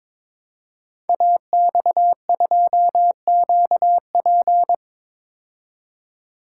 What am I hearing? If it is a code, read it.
Morse code, heard as a keyed tone, about 22 wpm, 700 Hz. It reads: AX2QP
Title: Code Groups - 5 Letter:Numbers - 22wpm